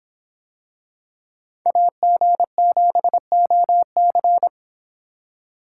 Morse code, heard as a keyed tone, about 26 wpm, 700 Hz. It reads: AG7OC